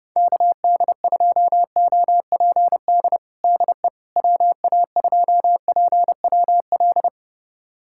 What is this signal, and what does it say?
Morse code, heard as a keyed tone, about 30 wpm, 700 Hz. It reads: KD2OPD DE WA2PWL
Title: USA Callsign DE Callsign 30wpm